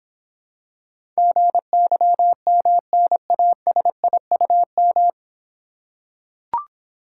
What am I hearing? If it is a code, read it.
Morse code, heard as a keyed tone, about 25 wpm, 700 Hz. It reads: GYMNASIUM
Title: Difficult Words - Speed-Racing with Hints - 20wpm